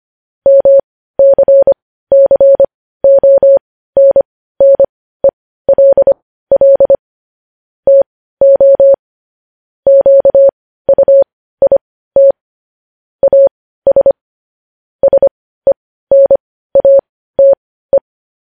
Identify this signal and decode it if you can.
Morse code, heard as a keyed tone, about 25 wpm, spaced as for 15 wpm, 550 Hz. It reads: MCCONNELL TO QUIT AS SENATE